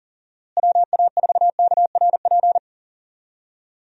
Morse code, heard as a keyed tone, about 40 wpm, 700 Hz. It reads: WA4KRP
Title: USA Call Signs - 6 Characters - Repeated 3x - 40wpm